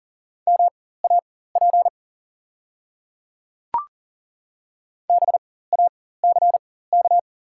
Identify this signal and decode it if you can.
Morse code, heard as a keyed tone, about 39 wpm, spaced as for 19 wpm, 700 Hz. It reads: MAP BACK
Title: 4x Character Spacing - Top 500 Words - 40wpm